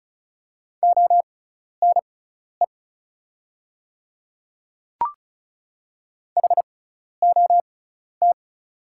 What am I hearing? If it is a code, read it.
Morse code, heard as a keyed tone, about 36 wpm, spaced as for 12 wpm, 690 Hz. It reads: ONE HOT
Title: 6x Character Spacing - Top 100 Words - 35wpm